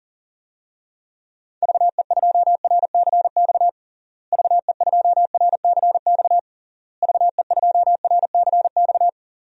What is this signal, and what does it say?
Morse code, heard as a keyed tone, about 37 wpm, 700 Hz. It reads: VE2RCX VE2RCX VE2RCX
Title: DXCC-100 Call Signs - 6 Characters - Repeated 3x - 40wpm